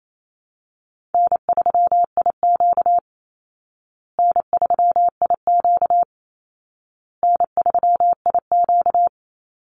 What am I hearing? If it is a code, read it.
Morse code, heard as a keyed tone, about 28 wpm, 700 Hz. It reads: N3IQ N3IQ N3IQ
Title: Easy Call Signs Repeated 3x - 28wpm